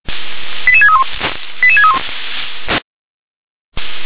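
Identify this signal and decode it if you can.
Five-tone selective call (ZVEI), heard as quick runs of five tones, tones of about 70 ms. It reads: 88631 88631
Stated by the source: ZVEI five-tone call received on SDR.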